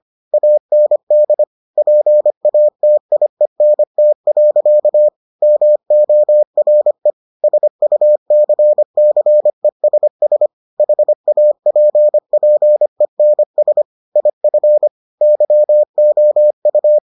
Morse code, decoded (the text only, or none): AND PATIENT. MORE SUCCESS HAPPENS IF YOU